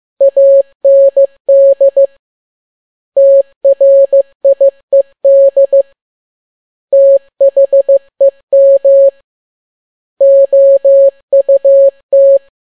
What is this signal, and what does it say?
Morse code, heard as a keyed tone, about 15 wpm, 550 Hz. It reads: AND TRIED THEM OUT
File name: n6tt p1 15C13